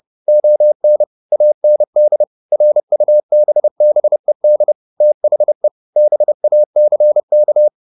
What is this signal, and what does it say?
Morse code, heard as a keyed tone, about 30 wpm, 600 Hz. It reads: ON AND RUBBED THE BACK